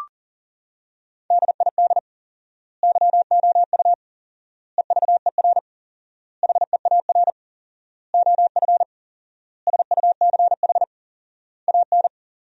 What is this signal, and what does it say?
Morse code, heard as a keyed tone, about 37 wpm, 700 Hz. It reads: DID YOU EVER HEAR OF SUCH AN
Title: 4x Word-Spacing - Sentences from Top 400 Words - 40wpm